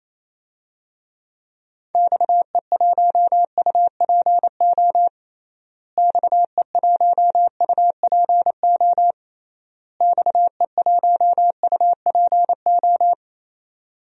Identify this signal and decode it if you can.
Morse code, heard as a keyed tone, about 28 wpm, 700 Hz. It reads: XE1UPO XE1UPO XE1UPO
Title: DXCC-100 Call Signs - 6 Characters - Repeated 3x - 28wpm